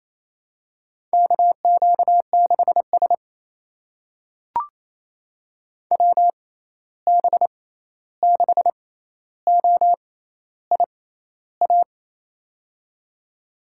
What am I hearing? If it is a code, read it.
Morse code, heard as a keyed tone, about 28 wpm, spaced as for 13 wpm, 700 Hz. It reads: KQ6S WB6OIA
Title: USA Call Signs - 6x Character Spacing - 28wpm